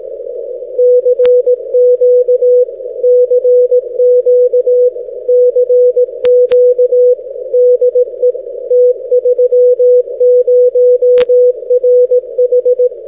Morse code, heard as a keyed tone, about 18 wpm, 500 Hz. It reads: CQ CQ CQ DE T30RH
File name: t30rh- -x